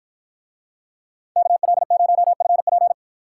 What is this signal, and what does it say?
Morse code, heard as a keyed tone, about 53 wpm, 700 Hz. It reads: KL0FP